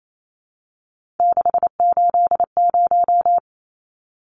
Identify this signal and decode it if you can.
Morse code, heard as a keyed tone, about 28 wpm, 700 Hz. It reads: 680
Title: Numbers - 3-Digit 28wpm